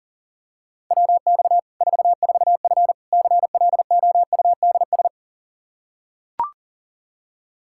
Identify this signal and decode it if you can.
Morse code, heard as a keyed tone, about 40 wpm, 700 Hz. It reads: WX 44F CLOUDS
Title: K1USN SST QSO Practice 40wpm